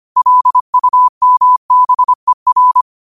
Morse code, heard as a keyed tone, about 25 wpm, 1000 Hz. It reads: LUMBER
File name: qotdfiles / 25wpm